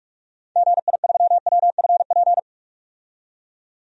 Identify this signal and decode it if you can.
Morse code, heard as a keyed tone, about 46 wpm, 700 Hz. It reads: GI3WFP
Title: DXCC-100 Call Signs - 6 Characters - 45wpm